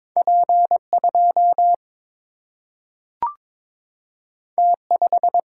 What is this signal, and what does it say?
Morse code, heard as a keyed tone, about 22 wpm, 700 Hz. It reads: P 2 T 5
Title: Single-Letter Single-Number 22wpm